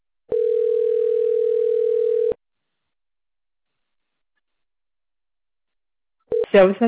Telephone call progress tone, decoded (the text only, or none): ringback tone